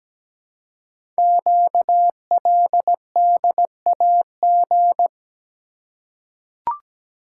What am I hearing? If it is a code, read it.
Morse code, heard as a keyed tone, about 17 wpm, 700 Hz. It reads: QLDAG
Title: Code Groups - 5 Letters - 17wpm